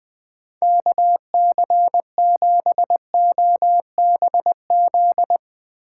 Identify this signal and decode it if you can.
Morse code, heard as a keyed tone, about 22 wpm, 700 Hz. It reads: KC7OBZ
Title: Speed Racing - USA 2x3 Call Signs 20wpm